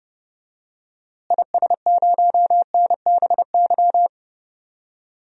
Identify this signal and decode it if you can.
Morse code, heard as a keyed tone, about 28 wpm, 700 Hz. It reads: IS0NBY
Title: DXCC-100 Call Signs - 6 Characters - Repeated 3x - 30wpm